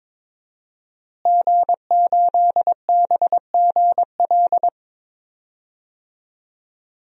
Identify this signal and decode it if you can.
Morse code, heard as a keyed tone, about 21 wpm, 700 Hz. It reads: G8BGL
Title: DXCC-100 Call Signs - 5 Characters - Repeated 3x - 22wpm